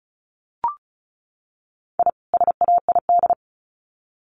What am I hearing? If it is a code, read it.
Morse code, heard as a keyed tone, about 34 wpm, 700 Hz. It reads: I SAID
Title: Sets of 2 Words 35wpm